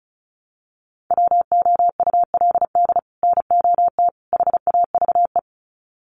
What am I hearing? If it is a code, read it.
Morse code, heard as a keyed tone, about 34 wpm, 700 Hz. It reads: WOULD NOT HAVE
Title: Sets of 3 Words 35wpm